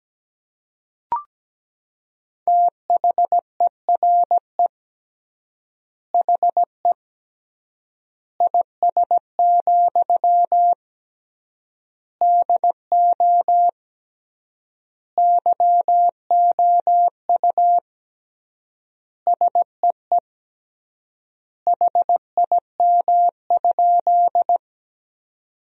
Morse code, heard as a keyed tone, about 17 wpm, 700 Hz. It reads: THERE HE IS, DO YOU SEE HIM?
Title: 3x Word-Spacing - Sentences from Top 100 Words - 17wpm